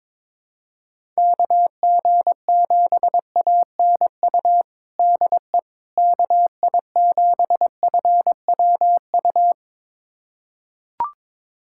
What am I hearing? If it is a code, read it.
Morse code, heard as a keyed tone, about 21 wpm, 700 Hz. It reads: KG7ANU DE KI7FWU
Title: USA Callsign DE Callsign - 6 Characters - 22wpm